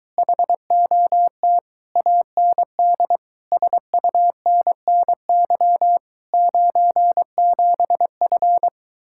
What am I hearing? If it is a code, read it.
Morse code, heard as a keyed tone, about 22 wpm, 700 Hz. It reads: HOT AND SUNNY 97F